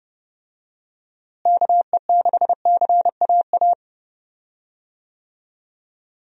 Morse code, heard as a keyed tone, about 30 wpm, 700 Hz. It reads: KE6CAA
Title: USA Call Signs - 6 Characters - Repeated 3x - 30wpm